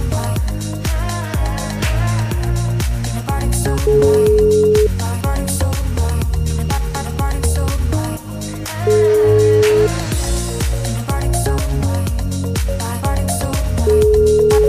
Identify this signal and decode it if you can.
Ringback tone (Europe, 425 Hz), with noise.